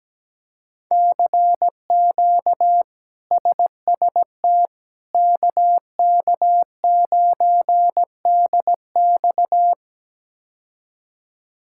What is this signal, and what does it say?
Morse code, heard as a keyed tone, about 17 wpm, 700 Hz. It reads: CQ SST KK9DX
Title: K1USN SST QSO Practice - 17wpm